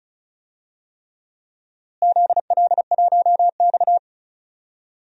Morse code, heard as a keyed tone, about 34 wpm, 700 Hz. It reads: ZL1X
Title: DXCC-100 Call Signs - 4 Characters - Repeated 3x - 35wpm